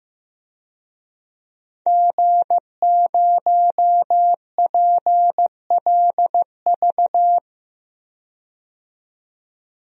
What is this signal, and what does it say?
Morse code, heard as a keyed tone, about 15 wpm, 700 Hz. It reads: G0PLV
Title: DXCC-100 Call Signs - 5 Characters - 15wpm